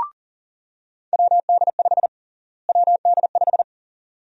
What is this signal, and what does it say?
Morse code, heard as a keyed tone, about 40 wpm, 700 Hz. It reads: WD5 WD5
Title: USA Call Sign Prefixes - Repeated 2x - 40wpm